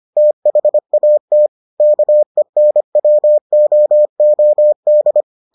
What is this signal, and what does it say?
Morse code, heard as a keyed tone, about 25 wpm, 600 Hz. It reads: THAT KENWOOD